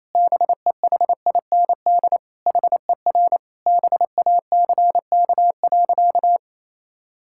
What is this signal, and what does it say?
Morse code, heard as a keyed tone, about 27 wpm, 700 Hz. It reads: BEHIND HER BACK.
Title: Sentences from Top 1000 Words - Review Part 6 - 28wpm